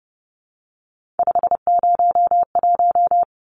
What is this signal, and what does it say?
Morse code, heard as a keyed tone, about 30 wpm, 700 Hz. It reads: 501
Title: Numbers - 3-Digit 30wpm